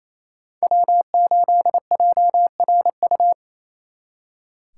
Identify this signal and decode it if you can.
Morse code, heard as a keyed tone, about 33 wpm, 700 Hz. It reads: W8JRU
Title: Speed Racing - USA 1x3 Call Signs 28wpm